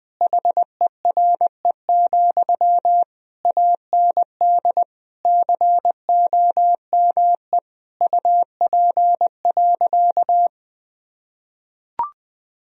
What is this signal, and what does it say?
Morse code, heard as a keyed tone, about 21 wpm, 700 Hz. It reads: HERE, AND COME UP.
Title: Sentences from Top 1000 Words - Review Part 2 - 20wpm